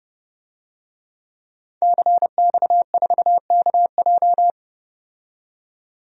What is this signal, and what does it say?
Morse code, heard as a keyed tone, about 30 wpm, 700 Hz. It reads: CX4KJ
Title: DXCC-100 Call Signs - 5 Characters - 30wpm